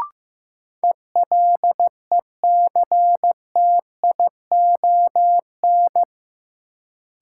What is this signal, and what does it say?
Morse code, heard as a keyed tone, about 15 wpm, 700 Hz. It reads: ELECTION
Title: Sentences from Top 1500 Words - New Word Drill Alt - 15wpm